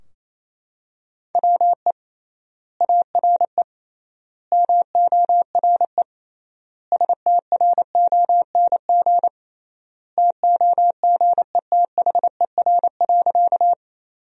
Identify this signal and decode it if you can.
Morse code, heard as a keyed tone, about 28 wpm, 700 Hz. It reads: WE ARE MORE STRONG TOGETHER.
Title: 3x Word-Spacing - Sentences from Top 500 Words - 28wpm